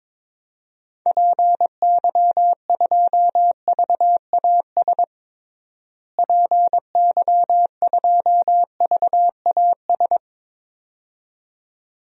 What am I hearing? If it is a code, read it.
Morse code, heard as a keyed tone, about 22 wpm, 700 Hz. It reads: PY2VAS PY2VAS
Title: DXCC-100 Call Signs - 6 Characters - Repeated 2x - 22wpm